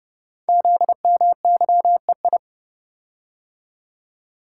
Morse code, heard as a keyed tone, about 30 wpm, 700 Hz. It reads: ZMYEI
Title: Code Groups - 5 Letters - 30wpm